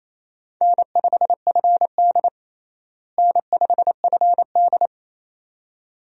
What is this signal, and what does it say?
Morse code, heard as a keyed tone, about 28 wpm, 700 Hz. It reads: N5FD N5FD
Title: USA Call Signs - 4 Characters - Repeated 2x - 28wpm